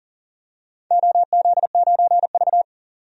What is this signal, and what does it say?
Morse code, heard as a keyed tone, about 40 wpm, 700 Hz. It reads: OZ9V
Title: DXCC-100 Call Signs - 4 Characters - 40wpm